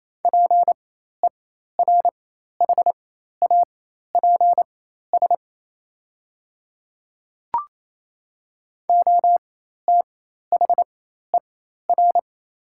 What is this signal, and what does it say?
Morse code, heard as a keyed tone, about 28 wpm, spaced as for 13 wpm, 700 Hz. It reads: PERHAPS OTHER
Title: 4x Character Spacing - Top 500 Words - 28wpm